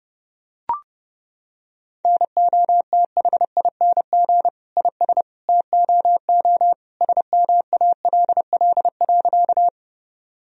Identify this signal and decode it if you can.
Morse code, heard as a keyed tone, about 30 wpm, 700 Hz. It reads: NOTHING IS TOO SMALL.
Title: Sentences from Top 1000 Words - Review Part 1 - 30wpm